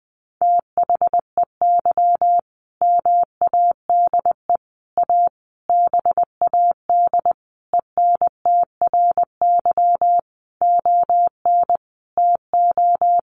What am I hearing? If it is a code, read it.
Morse code, heard as a keyed tone, about 20 wpm, 700 Hz. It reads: THEY MADE A BAD ENTRY ON TO
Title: Sentences from Top 1500 Words 20wpm